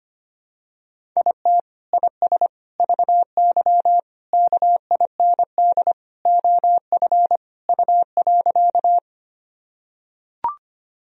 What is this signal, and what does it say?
Morse code, heard as a keyed tone, about 25 wpm, 700 Hz. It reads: IT IS VY KIND OF U.
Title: Sentences from Top 200 Words and Abbreviations 25wpm